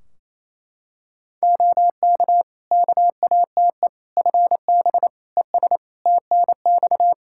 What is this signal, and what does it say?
Morse code, heard as a keyed tone, about 28 wpm, 700 Hz. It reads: OK KATE FB ES TNX
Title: Practice QSO - Exchange 2 - Info - 28wpm